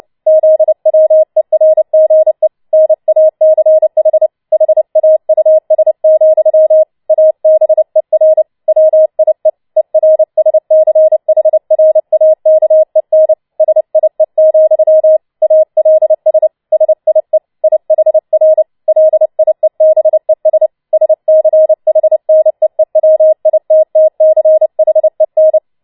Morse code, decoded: ZWERGE NACH HAUS, ABER WIE ERSCHRAKEN SIE, ALS SIE IHR LIEBES SCHNEEWITTCHEN

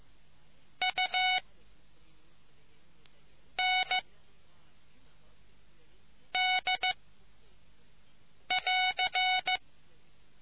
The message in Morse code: U N D <AR>

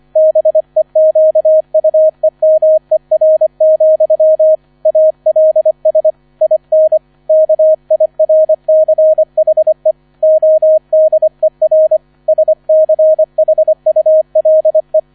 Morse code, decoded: BEQUEMER, ALS IN KIRCHE ODER SCHULE

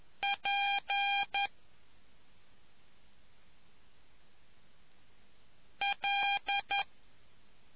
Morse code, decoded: P L

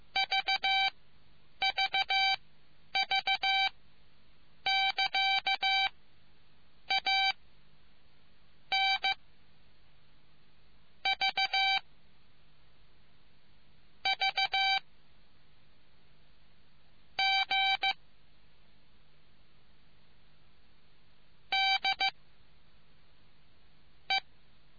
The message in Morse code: VVV <KA> A N V V G D E

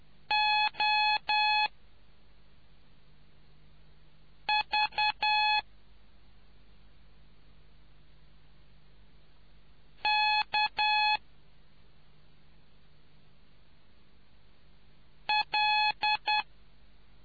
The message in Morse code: O V K L

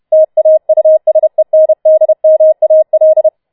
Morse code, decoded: TAUSENDMAL

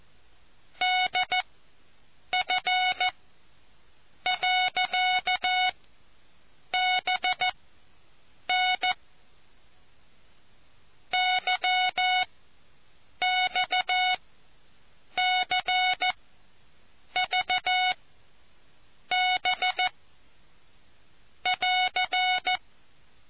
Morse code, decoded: DF.BN YXCVB <AR>

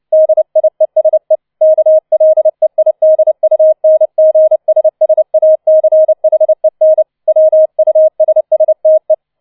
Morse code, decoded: DIESE KLEIDUNGSSACHEN WUSSTE

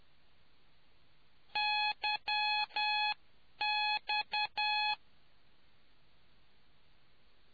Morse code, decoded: YX

